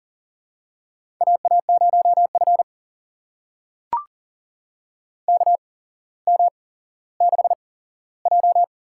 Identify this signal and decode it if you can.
Morse code, heard as a keyed tone, about 40 wpm, spaced as for 17 wpm, 700 Hz. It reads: AA0F KM6J
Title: USA Call Signs - 4 Characters - 8x Character Spacing - 40wpm